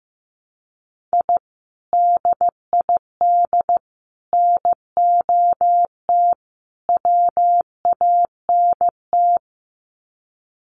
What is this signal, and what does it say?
Morse code, heard as a keyed tone, about 15 wpm, 700 Hz. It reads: I DID NOT WANT
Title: Sets of 4 Words 15wpm